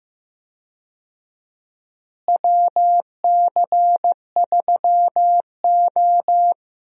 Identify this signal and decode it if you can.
Morse code, heard as a keyed tone, about 15 wpm, 700 Hz. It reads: WC3O
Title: USA Call Signs - 4 Characters - Repeated 2x - 15wpm